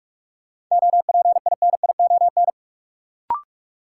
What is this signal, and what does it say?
Morse code, heard as a keyed tone, about 46 wpm, 700 Hz. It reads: OPINION
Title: Sentences from Top 1200 Words - New Word Drill - 45wpm